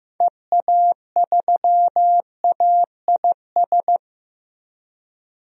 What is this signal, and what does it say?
Morse code, heard as a keyed tone, about 15 wpm, 700 Hz. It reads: EA3AIS
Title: DXCC-100 Call Signs - 6 Characters - Repeated 2x - 15wpm